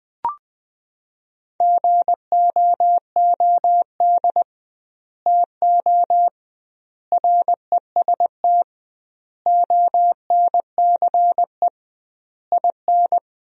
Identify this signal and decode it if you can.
Morse code, heard as a keyed tone, about 20 wpm, 700 Hz. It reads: GOOD TO REST ONCE IN
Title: 2x Word-Spacing - Sentences from Top 500 Words - 20wpm